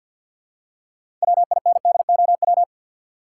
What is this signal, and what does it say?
Morse code, heard as a keyed tone, about 51 wpm, 700 Hz. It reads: WINDOW
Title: Six Letter Words - 50wpm